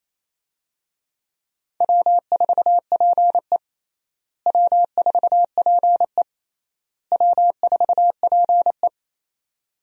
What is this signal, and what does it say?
Morse code, heard as a keyed tone, about 28 wpm, 700 Hz. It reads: W4PE W4PE W4PE
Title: USA Call Signs - 4 Characters - Repeated 3x - 28wpm